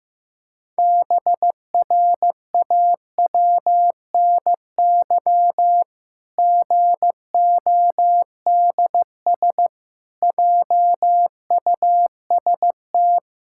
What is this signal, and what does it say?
Morse code, heard as a keyed tone, about 15 wpm, 700 Hz. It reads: BRAWNY GODS JUST
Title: Pangram Challenge v2 - 15wpm